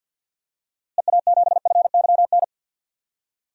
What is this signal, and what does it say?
Morse code, heard as a keyed tone, about 53 wpm, 700 Hz. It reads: EA7FYN
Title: DXCC-100 Call Signs - 6 Characters - Repeated 2x - 50wpm